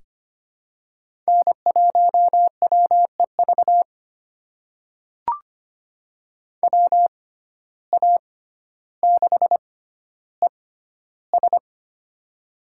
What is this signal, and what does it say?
Morse code, heard as a keyed tone, about 25 wpm, spaced as for 13 wpm, 700 Hz. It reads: N1WEV WA6ES
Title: USA Call Signs - 5 Characters - 6x Character Spacing - 25wpm